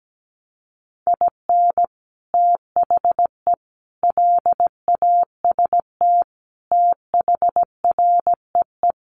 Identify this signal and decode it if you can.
Morse code, heard as a keyed tone, about 17 wpm, 700 Hz. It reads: IN THE LAST THREE
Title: Sets of 5 Words Encore2 17wpm